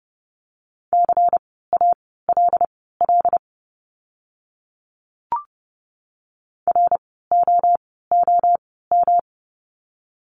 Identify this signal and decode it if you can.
Morse code, heard as a keyed tone, about 30 wpm, spaced as for 17 wpm, 700 Hz. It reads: CALL ROOM
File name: course / 30:10wpm / 115 Lesson 19f - Letter w - Words 30wpm